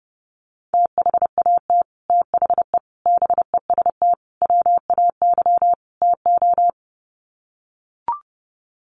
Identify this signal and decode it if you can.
Morse code, heard as a keyed tone, about 30 wpm, 700 Hz. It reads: THAT THE BEST WAY TO
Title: Sets of 5 Words Encore2 30wpm